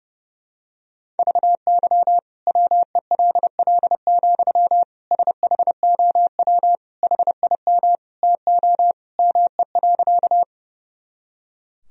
Morse code, heard as a keyed tone, about 32 wpm, 700 Hz. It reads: VY WELL, SHOW HIM TO ME.
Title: Sentences from Top 200 Words and Abbreviations 30wpm